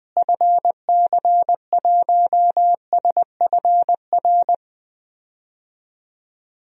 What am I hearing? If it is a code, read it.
Morse code, heard as a keyed tone, about 20 wpm, 700 Hz. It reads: FC1SFR
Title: DXCC-100 Call Signs - 6 Characters - Repeated 3x - 20wpm